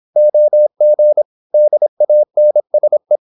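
Morse code, heard as a keyed tone, about 26 wpm, 600 Hz. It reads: OG DANSE
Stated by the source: High-speed (QRQ) Morse at 26 wpm.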